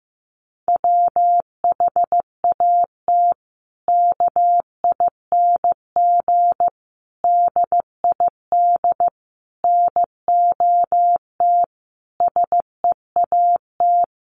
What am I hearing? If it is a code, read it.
Morse code, heard as a keyed tone, about 15 wpm, 700 Hz. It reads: WHAT KING DID NOT SEAT
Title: Sentences from Top 1500 Words 15wpm